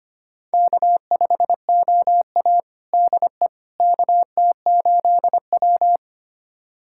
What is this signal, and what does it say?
Morse code, heard as a keyed tone, about 25 wpm, 700 Hz. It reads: K5OA DE KT8W
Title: USA Callsign DE Callsign - 4 Characters - 25wpm